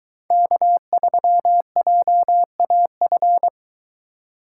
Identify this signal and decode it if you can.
Morse code, heard as a keyed tone, about 22 wpm, 700 Hz. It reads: K3JAF